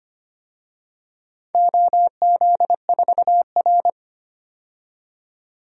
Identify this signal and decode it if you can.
Morse code, heard as a keyed tone, about 25 wpm, 700 Hz. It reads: OZ4R
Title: DXCC-100 Call Signs - 4 Characters - 25wpm